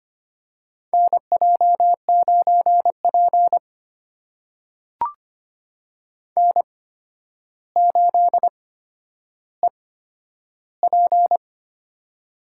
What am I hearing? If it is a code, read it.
Morse code, heard as a keyed tone, about 25 wpm, spaced as for 11 wpm, 700 Hz. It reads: NJ9P N8EP
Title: USA Call Signs - 4 Characters - 8x Character Spacing - 25wpm